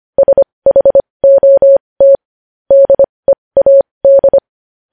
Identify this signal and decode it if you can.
Morse code, heard as a keyed tone, about 25 wpm, spaced as for 20 wpm, 550 Hz. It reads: SHOT DEAD
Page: news-headlines-in-morse-code-at-20-wpm